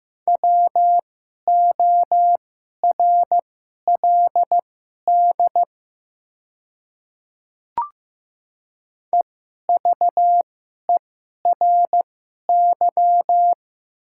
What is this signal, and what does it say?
Morse code, heard as a keyed tone, about 15 wpm, spaced as for 11 wpm, 700 Hz. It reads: WORLD EVERY
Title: 2x Character Spacing - Top 200 Words - 15wpm